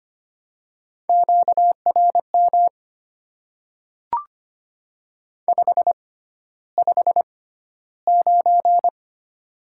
Morse code, heard as a keyed tone, about 25 wpm, spaced as for 13 wpm, 700 Hz. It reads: QRM 559
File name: Top 90 QSO Elements - 6x Character Spacing - 25wpm